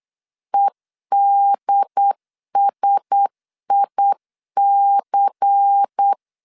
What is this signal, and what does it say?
Morse code, heard as a keyed tone, about 8 wpm, 800 Hz. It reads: EDSIC
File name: LEZIONI / 6 PKX0 / 6 PKX0 8wpm 3341